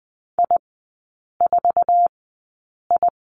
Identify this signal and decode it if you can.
Morse code, heard as a keyed tone, about 20 wpm, spaced as for 8 wpm, 700 Hz. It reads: I4I
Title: 047 Lesson 9e - Number 4 - Three Characters 20wpm